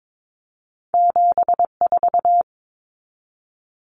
Morse code, heard as a keyed tone, about 22 wpm, 700 Hz. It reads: 74
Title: Common Cut Numbers - 2-Digit 22wpm